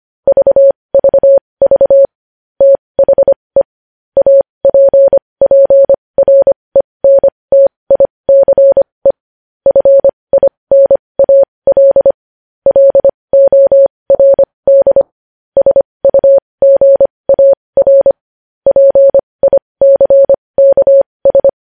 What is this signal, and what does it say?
Morse code, heard as a keyed tone, about 25 wpm, spaced as for 20 wpm, 550 Hz. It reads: VVV THE APPRENTICE FINAL LORD SUGAR PICKS
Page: News Headlines in Morse Code at 20 WPM